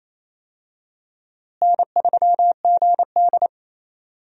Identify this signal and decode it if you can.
Morse code, heard as a keyed tone, about 28 wpm, 700 Hz. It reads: N3GD
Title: USA Call Signs - 4 Characters - Repeated 3x - 28wpm